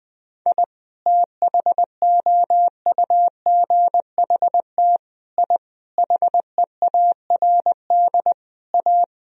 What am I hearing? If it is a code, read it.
Morse code, heard as a keyed tone, about 21 wpm, 700 Hz. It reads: I THOUGHT I HEARD A
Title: Sentences from Top 1000 Words - Review Part 1 - 20wpm